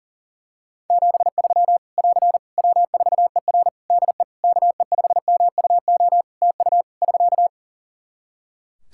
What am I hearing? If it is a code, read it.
Morse code, heard as a keyed tone, about 40 wpm, 700 Hz. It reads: 73 <AR> W4ER DE KE5MUO TU <SK>
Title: K1USN SST QSO Practice 40wpm